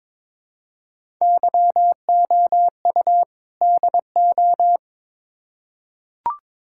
Morse code, heard as a keyed tone, about 22 wpm, 700 Hz. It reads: YOU DO
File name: Build A Phrase - 2-Word Phrases - 22wpm